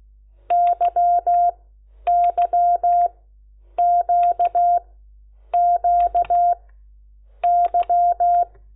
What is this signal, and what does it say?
Morse code, heard as a keyed tone, about 16 wpm, spaced as for 8 wpm, 700 Hz. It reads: YYQQY
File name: Lecciones / CLASE Nº10